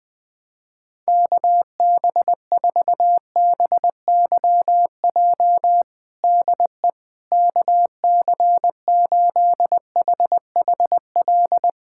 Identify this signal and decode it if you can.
Morse code, heard as a keyed tone, about 20 wpm, 700 Hz. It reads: KB4BYJ DE KC8HHL
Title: USA Callsign DE Callsign 20wpm